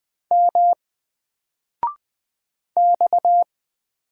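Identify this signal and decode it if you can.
Morse code, heard as a keyed tone, about 20 wpm, 700 Hz. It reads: M X
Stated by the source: Single letter and number Morse drill at 20 wpm, from N4LCD.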